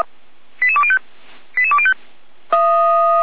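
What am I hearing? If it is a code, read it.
Five-tone selective call (ZVEI), heard as quick runs of five tones, tones of about 80 ms. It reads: 88205 88205
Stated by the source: ZVEI five-tone selective call received on an SDR.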